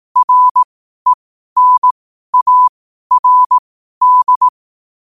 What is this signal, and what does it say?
Morse code, heard as a keyed tone, about 18 wpm, spaced as for 12 wpm, 1000 Hz. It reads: RENARD
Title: 12 WPM morse code quotes for Thu, 29 Feb 2024